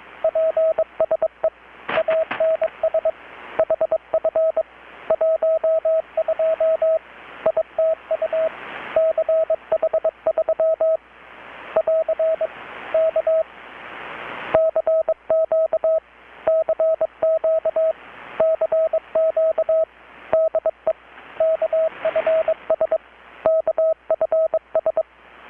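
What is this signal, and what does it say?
Morse code, heard as a keyed tone, about 22 wpm, 640 Hz, with radio noise. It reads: PSE ANS HF 12 ITU CH3 <AR> K CQ CQ CQ DE KFS KFS
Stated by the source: Morse from KPH's Night of Nights XX maritime radiotelegraphy event, received in Chiapas, Mexico about 3,800 km away on a Kenwood R-600 with a grounded 10 m long-wire antenna.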